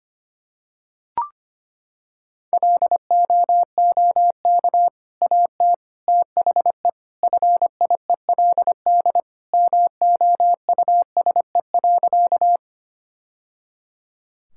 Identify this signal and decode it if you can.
Morse code, heard as a keyed tone, about 25 wpm, 700 Hz. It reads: LOOK AT THE FIELD MOUSE.
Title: Sentences from Top 1000 Words - Review Part 5 - 25wpm